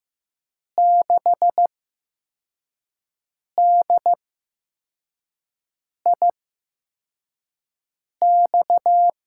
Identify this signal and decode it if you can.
Morse code, heard as a keyed tone, about 15 wpm, 700 Hz. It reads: 6 D I X
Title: USA Call Signs - 5 Characters - 8x Character Spacing - 15wpm